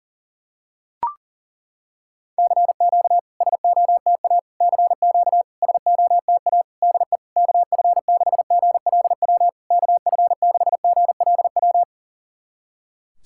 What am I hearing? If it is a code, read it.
Morse code, heard as a keyed tone, about 39 wpm, 690 Hz. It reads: CQ SOTA CQ SOTA DE KF6GLW KF6GLW
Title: SOTA QSO Practice 40wpm